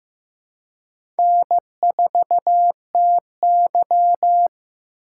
Morse code, heard as a keyed tone, about 15 wpm, 700 Hz. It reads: N4TY
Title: USA Call Signs - 3x Character Spacing - 15wpm